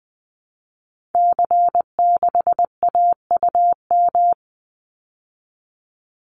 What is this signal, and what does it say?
Morse code, heard as a keyed tone, about 20 wpm, 700 Hz. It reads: C6AUM
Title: Call Signs Repeated 3x - 20wpm